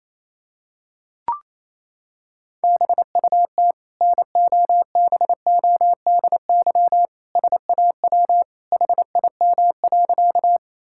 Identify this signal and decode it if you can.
Morse code, heard as a keyed tone, about 28 wpm, 700 Hz. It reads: BUT NOBODY SAW HIM.
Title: Sentences from Top 1600 Words 28wpm